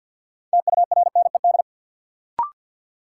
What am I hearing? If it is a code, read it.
Morse code, heard as a keyed tone, about 50 wpm, 700 Hz. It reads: TURNED
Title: Sets of 4 Words - New Word Drill - 50wpm